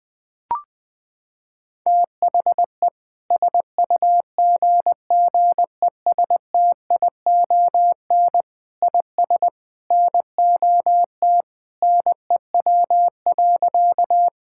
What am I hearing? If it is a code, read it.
Morse code, heard as a keyed tone, about 20 wpm, 700 Hz. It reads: THE SUGGESTION IS NOT NEW.
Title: Sentences from Top 1600 Words 20wpm